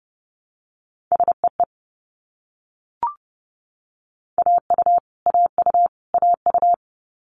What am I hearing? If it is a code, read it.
Morse code, heard as a keyed tone, about 30 wpm, 700 Hz. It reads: SEE AU AU AU
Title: Multi-letter Phonemes Repeated 3x 30wpm